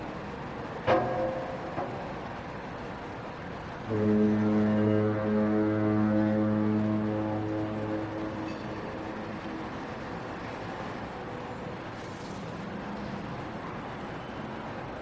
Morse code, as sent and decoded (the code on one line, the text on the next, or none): none